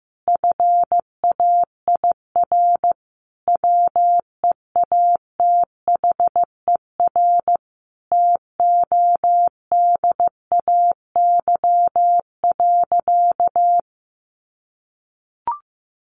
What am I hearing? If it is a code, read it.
Morse code, heard as a keyed tone, about 15 wpm, 700 Hz. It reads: FAIR WEATHER TODAY.
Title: Sentences from Top 1300 Words 15wpm